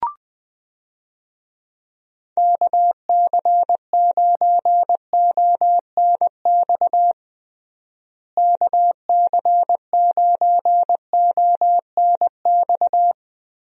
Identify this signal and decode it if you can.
Morse code, heard as a keyed tone, about 21 wpm, 700 Hz. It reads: KC9ONX KC9ONX
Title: USA Call Signs - 6 Characters - Repeated 3x - 20wpm